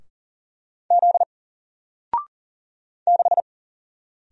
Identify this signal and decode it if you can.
Morse code, heard as a keyed tone, about 38 wpm, 700 Hz. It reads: Z 6
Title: Single Character (Letter:Number:Punctuation:Prosign) Alt 40wpm